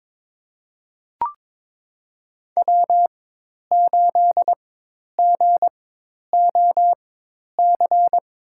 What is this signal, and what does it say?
Morse code, heard as a keyed tone, about 22 wpm, spaced as for 10 wpm, 700 Hz. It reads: W8GOC
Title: USA Call Signs - 5 Characters - 4x Character Spacing - 22wpm